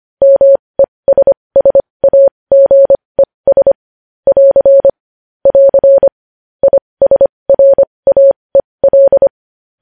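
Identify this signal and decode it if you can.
Morse code, heard as a keyed tone, about 25 wpm, spaced as for 20 wpm, 550 Hz. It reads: MESSAGES <AR> <AR> ISRAEL
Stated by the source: Morse code news headlines sent at 20.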